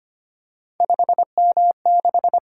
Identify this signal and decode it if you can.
Morse code, heard as a keyed tone, about 25 wpm, 700 Hz. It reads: 5M6
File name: Three Letter-Number Combinations - 25wpm